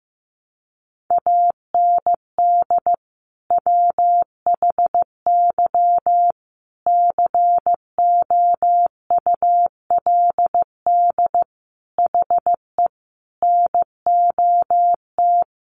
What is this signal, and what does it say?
Morse code, heard as a keyed tone, about 15 wpm, 700 Hz. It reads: AND WHY COULD HE NOT
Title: Sentences from Top 600 Words 15wpm